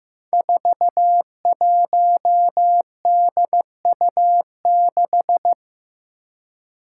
Morse code, heard as a keyed tone, about 15 wpm, 700 Hz. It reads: 41DU6
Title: Five Challenging Letter-Number Combinations 15wpm